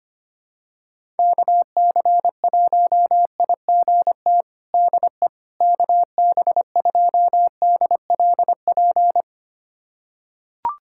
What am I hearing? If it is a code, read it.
Morse code, heard as a keyed tone, about 25 wpm, 700 Hz. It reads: KC1IGT DE KB2DLP
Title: USA Callsign DE Callsign - 6 Characters - 25wpm